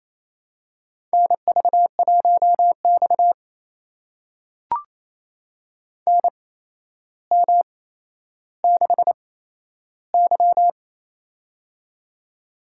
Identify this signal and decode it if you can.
Morse code, heard as a keyed tone, about 28 wpm, spaced as for 12 wpm, 700 Hz. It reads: NV1X NM6Y
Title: USA Call Signs - 4 Characters - 8x Character Spacing - 28wpm